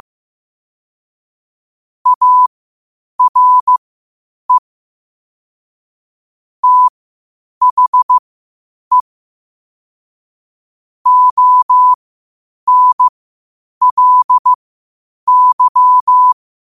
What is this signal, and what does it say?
Morse code, heard as a keyed tone, about 15 wpm, spaced as for 8 wpm, 1000 Hz. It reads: ARE THE ONLY
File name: qotdfiles / 8wpm